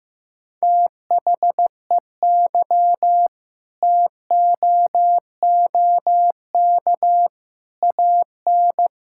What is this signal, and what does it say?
Morse code, heard as a keyed tone, about 15 wpm, 700 Hz. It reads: THEY TOOK AN
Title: Sentences from Top 1000 Words - Review Part 6 - 15wpm